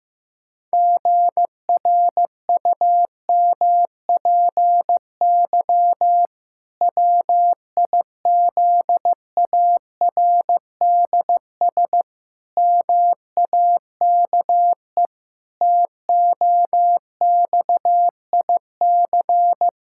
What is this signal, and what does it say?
Morse code, heard as a keyed tone, about 15 wpm, 700 Hz. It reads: GRUMPY WIZARDS MAKE TOXIC
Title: Pangram Challenge v2 - 15wpm